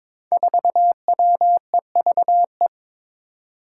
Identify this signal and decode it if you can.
Morse code, heard as a keyed tone, about 22 wpm, 700 Hz. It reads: 4WEVE
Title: Five Challenging Letter-Number Combinations 22wpm